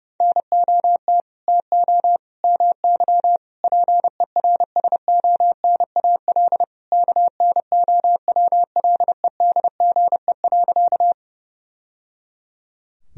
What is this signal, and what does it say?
Morse code, heard as a keyed tone, about 30 wpm, 700 Hz. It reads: NOT TO MY PERSONAL KNOWLEDGE.
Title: Sentences from Top 1000 Words - Review Part 4 - 30wpm